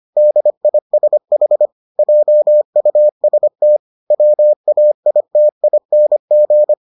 Morse code, heard as a keyed tone, about 25 wpm, 600 Hz. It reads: DISH JUST WAITING